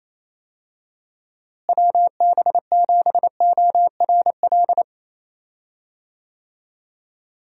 Morse code, heard as a keyed tone, about 28 wpm, 700 Hz. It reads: WB7ORL
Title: USA Call Signs - 6 Characters - Repeated 3x - 28wpm